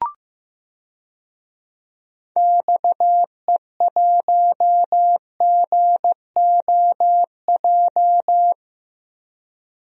Morse code, heard as a keyed tone, about 15 wpm, 700 Hz. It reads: XE1GOJ
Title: DXCC-100 Call Signs - 6 Characters - 15wpm